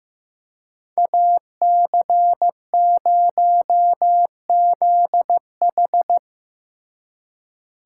Morse code, heard as a keyed tone, about 18 wpm, 700 Hz. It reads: AC0ZH
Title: Speed Racing - USA 2x2 Call Signs 15wpm